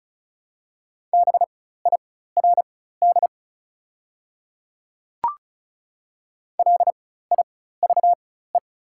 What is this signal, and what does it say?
Morse code, heard as a keyed tone, about 36 wpm, spaced as for 16 wpm, 700 Hz. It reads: BIRD LIVE
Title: 4x Character Spacing - Top 500 Words - 35wpm